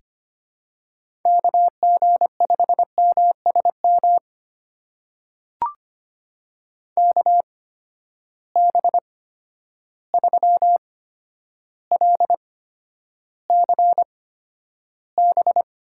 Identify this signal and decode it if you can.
Morse code, heard as a keyed tone, about 25 wpm, spaced as for 11 wpm, 700 Hz. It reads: KG5MSM KB3LCB
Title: USA Call Signs - 6 Characters - 8x Character Spacing - 25wpm